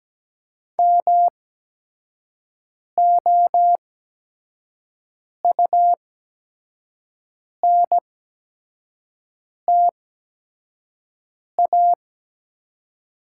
Morse code, heard as a keyed tone, about 17 wpm, spaced as for 5 wpm, 690 Hz. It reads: MOUNTA
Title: 8x Character Spacing - Top 300 Words - 17wpm